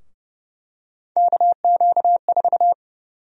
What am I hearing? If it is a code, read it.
Morse code, heard as a keyed tone, about 30 wpm, 700 Hz. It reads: KQ4
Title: USA Call Sign Prefixes - 30wpm